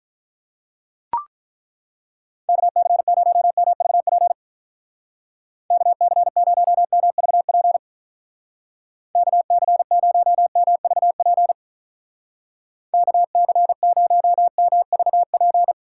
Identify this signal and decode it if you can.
Morse code, heard as a keyed tone, about 42 wpm, 700 Hz. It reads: KC0MVP KC0MVP KC0MVP KC0MVP
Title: Speed Racing - USA Call Signs 35wpm